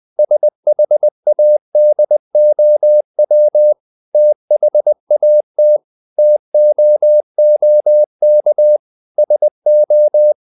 Morse code, heard as a keyed tone, about 20 wpm, 600 Hz. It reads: SHADOW THAT TOOK SO
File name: the old man and the sea part 2 20wpm